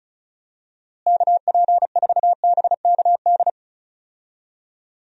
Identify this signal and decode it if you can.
Morse code, heard as a keyed tone, about 36 wpm, 700 Hz. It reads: KP4BKD
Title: DXCC-100 Call Signs - 6 Characters - Repeated 3x - 35wpm